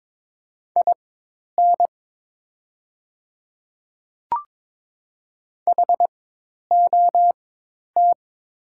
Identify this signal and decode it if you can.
Morse code, heard as a keyed tone, about 22 wpm, spaced as for 10 wpm, 700 Hz. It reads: IN HOT
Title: 4x Character Spacing - Top 100 Words - 22wpm